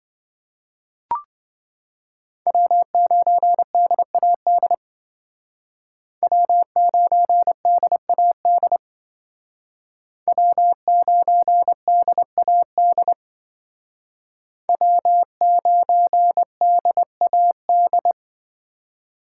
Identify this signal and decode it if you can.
Morse code, heard as a keyed tone, about 25 wpm, 700 Hz. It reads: W9DAD W9DAD W9DAD W9DAD